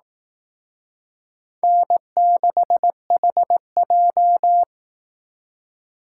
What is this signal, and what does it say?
Morse code, heard as a keyed tone, about 18 wpm, 700 Hz. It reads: N6HJ
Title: Speed Racing - USA 1x2 Call Signs 15wpm